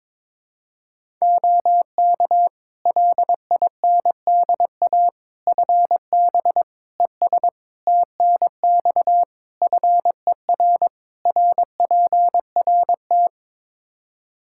Morse code, heard as a keyed tone, about 22 wpm, 700 Hz. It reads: OK LINDA FB ES TNX FER RPRT
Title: Practice QSO - Exchange 2 - Info - 22wpm